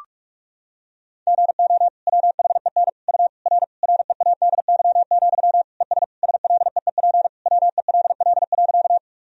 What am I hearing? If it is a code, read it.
Morse code, heard as a keyed tone, about 46 wpm, 700 Hz. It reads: GO WHEN U R READY, ES SLEEP WELL.
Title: Sentences from Top 500 Words and Abbreviations 45wpm